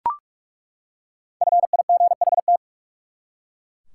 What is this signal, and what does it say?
Morse code, heard as a keyed tone, about 41 wpm, 700 Hz. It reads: FIGHT
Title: Five Letter Words - 45wpm